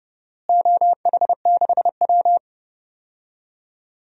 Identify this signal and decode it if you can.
Morse code, heard as a keyed tone, about 30 wpm, 700 Hz. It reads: OH6W
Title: DXCC-100 Call Signs - 4 Characters - Repeated 3x - 30wpm